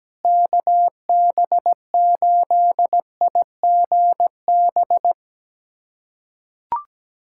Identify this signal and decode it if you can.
Morse code, heard as a keyed tone, about 17 wpm, 700 Hz. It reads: KB8IGB
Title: USA Callsign DE Callsign - 6 Characters - 17wpm